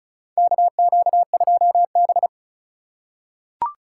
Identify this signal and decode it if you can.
Morse code, heard as a keyed tone, about 36 wpm, 700 Hz. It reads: KQ2B
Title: USA Call Signs - 4 Characters - 4x Character Spacing - 35wpm